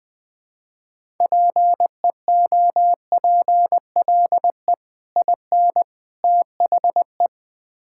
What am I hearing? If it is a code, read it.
Morse code, heard as a keyed tone, about 20 wpm, 700 Hz. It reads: PEOPLE IN THE
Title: Build A Phrase - 3-Word Phrases - 20wpm